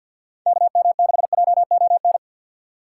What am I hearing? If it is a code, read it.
Morse code, heard as a keyed tone, about 48 wpm, 700 Hz. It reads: KM6JON
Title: USA Call Signs - 6 Characters - 6x Character Spacing - 50wpm